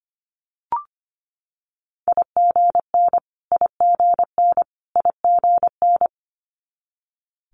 Morse code, heard as a keyed tone, about 25 wpm, 700 Hz. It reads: IGN IGN IGN
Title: Multi-letter Phonemes Repeated 3x 25wpm